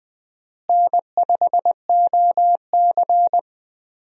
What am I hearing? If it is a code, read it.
Morse code, heard as a keyed tone, about 20 wpm, 700 Hz. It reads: N5OC
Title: USA Call Signs - 4 Characters - Repeated 2x - 20wpm